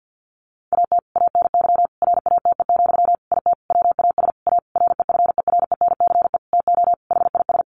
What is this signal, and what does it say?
Morse code, heard as a keyed tone, about 50 wpm, 700 Hz. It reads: AT ANY RATE, IT WAS A REFERENCE TO HIS
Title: Sentences from Top 1200 Words 50wpm